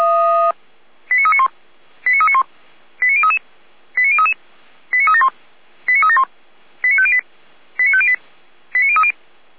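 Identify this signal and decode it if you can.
Five-tone selective call (ZVEI), heard as quick runs of five tones, tones of about 70 ms. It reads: 89391 89391 89033 89033 89371 89371 89508 89508 89030
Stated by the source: ZVEI five-tone selective call received on a software-defined radio.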